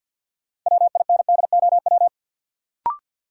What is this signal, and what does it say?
Morse code, heard as a keyed tone, about 50 wpm, 700 Hz. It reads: WINDOW
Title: Sentences from Top 1600 Words - New Word Drill Alt - 50wpm